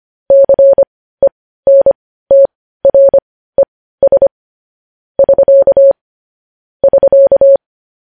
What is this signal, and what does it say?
Morse code, heard as a keyed tone, about 25 wpm, spaced as for 15 wpm, 550 Hz. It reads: CENTRES <SK> <SK>
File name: podcast / News headlines-1x-25x15